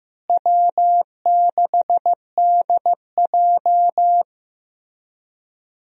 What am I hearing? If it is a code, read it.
Morse code, heard as a keyed tone, about 15 wpm, 700 Hz. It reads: W6DJ
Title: USA Call Signs - 4 Characters - Repeated 3x - 15wpm